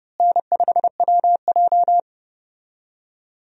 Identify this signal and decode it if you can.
Morse code, heard as a keyed tone, about 30 wpm, 700 Hz. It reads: N5WJ
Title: USA Call Signs - 4 Characters - Repeated 2x - 30wpm